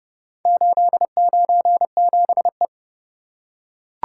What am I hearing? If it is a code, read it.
Morse code, heard as a keyed tone, about 29 wpm, 700 Hz. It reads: 897E